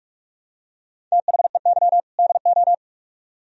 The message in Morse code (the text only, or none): THEY DO